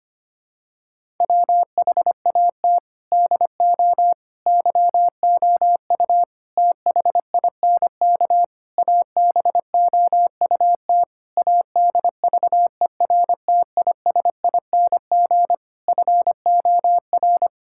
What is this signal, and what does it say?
Morse code, heard as a keyed tone, about 25 wpm, 700 Hz. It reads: WHAT DO YOU THINK ABOUT ADVERTISING FOR
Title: Sentences from Top 1000 Words - Review Part 6 - 25wpm